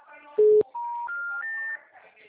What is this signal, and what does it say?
Special information tone.
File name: out- -2009- - - .10